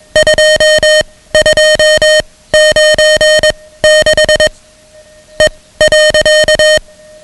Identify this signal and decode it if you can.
Morse code, heard as a keyed tone, about 21 wpm, spaced as for 16 wpm, 620 Hz, with noise. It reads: 2296 E.
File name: ham radio / CodeQuick / Code Quick 10